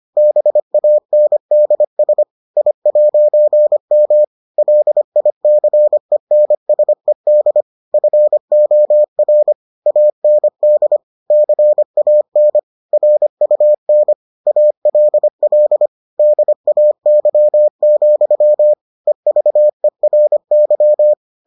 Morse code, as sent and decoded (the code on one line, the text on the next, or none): -... .- -. -.. ... / .. .----. -- / .-.. .. -.-. -. ... -.. / ..-. --- .-. / .- -. -.. / -.-. .- -. / .-. ..- -. / .- .-.. .-.. / -.. .- -.-- --..-- / . ...- . .-. -.--
BANDS I'M LICENSED FOR AND CAN RUN ALL DAY, EVERY